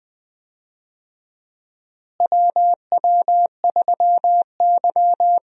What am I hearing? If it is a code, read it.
Morse code, heard as a keyed tone, about 21 wpm, 700 Hz. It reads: WW3Y
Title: USA Call Signs - 4 Characters - Repeated 3x - 20wpm